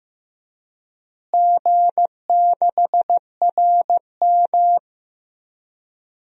Morse code, heard as a keyed tone, about 15 wpm, 700 Hz. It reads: G6RM
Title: DXCC-100 Call Signs - 4 Characters - Repeated 3x - 15wpm